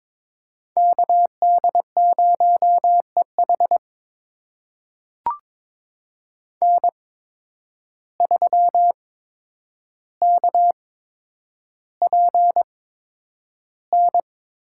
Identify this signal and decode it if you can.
Morse code, heard as a keyed tone, about 22 wpm, spaced as for 9 wpm, 700 Hz. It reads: KD0EH N3KPN
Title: USA Call Signs - 5 Characters - 8x Character Spacing - 22wpm